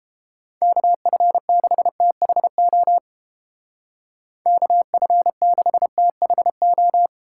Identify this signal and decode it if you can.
Morse code, heard as a keyed tone, about 30 wpm, 700 Hz. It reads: KF6THO KF6THO